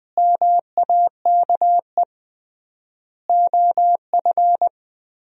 Morse code, heard as a keyed tone, about 21 wpm, 700 Hz. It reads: MAKE OF
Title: 3x Word-Spacing - Sentences from Top 100 Words - 20wpm